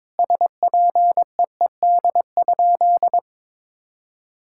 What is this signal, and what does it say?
Morse code, heard as a keyed tone, about 22 wpm, 700 Hz. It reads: SPEED?
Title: Sentences from Top 1000 Words - Review Part 5 - 22wpm